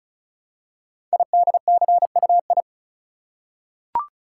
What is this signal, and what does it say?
Morse code, heard as a keyed tone, about 35 wpm, 700 Hz. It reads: IDCUI
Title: Code Groups - 5 Letter:Numbers - 35wpm